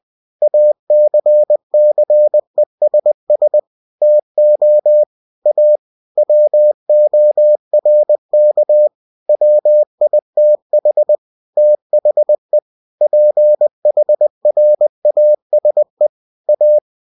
Morse code, heard as a keyed tone, about 20 wpm, 600 Hz. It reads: ACCESS TO A WORK WITH THE PHRASE A
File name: winnie-20wpm